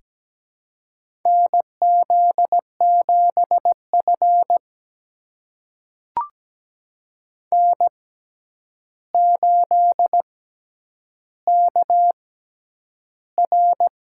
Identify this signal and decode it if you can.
Morse code, heard as a keyed tone, about 17 wpm, spaced as for 9 wpm, 700 Hz. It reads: NZ7F N8KR
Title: USA Call Signs - 4 Characters - 6x Character Spacing - 17wpm